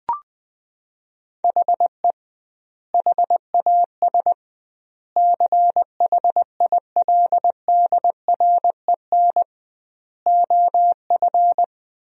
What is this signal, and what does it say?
Morse code, heard as a keyed tone, about 21 wpm, 700 Hz. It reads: HE HAS CHILDREN OF
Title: 2x Word-Spacing - Sentences from Top 300 Words - 20wpm